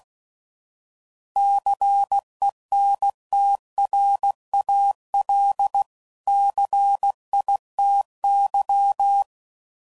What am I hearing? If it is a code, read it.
Morse code, heard as a keyed tone, about 16 wpm, 800 Hz. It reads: CENTRAL CITY
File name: CW Code Groups @15WPM